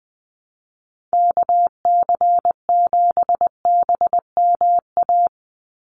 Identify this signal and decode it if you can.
Morse code, heard as a keyed tone, about 20 wpm, 700 Hz. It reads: KC7BMA
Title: USA Call Signs 20wpm